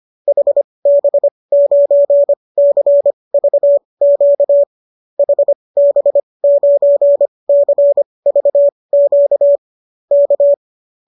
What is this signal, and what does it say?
Morse code, heard as a keyed tone, about 25 wpm, spaced as for 20 wpm, 570 Hz. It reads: HB9CVQ HB9CVQ K